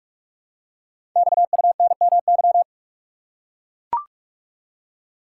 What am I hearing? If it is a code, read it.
Morse code, heard as a keyed tone, about 44 wpm, 690 Hz. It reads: XUNMY
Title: Code Groups - 5 Letters - 45wpm